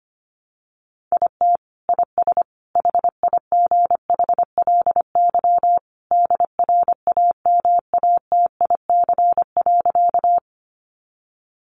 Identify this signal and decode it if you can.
Morse code, heard as a keyed tone, about 25 wpm, 700 Hz. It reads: IT IS HIGHLY DRAMATIC.
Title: Sentences from Top 1300 Words 25wpm